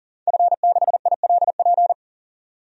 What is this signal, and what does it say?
Morse code, heard as a keyed tone, about 38 wpm, 700 Hz. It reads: F6ILP
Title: DXCC-100 Call Signs - 5 Characters - 40wpm